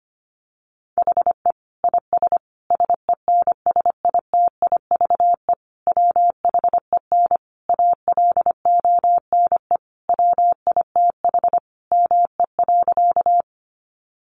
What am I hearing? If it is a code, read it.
Morse code, heard as a keyed tone, about 25 wpm, 700 Hz. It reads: HE IS SENSITIVE WHEN ALONE WITH ME.
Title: Sentences from Top 1400 Words 25wpm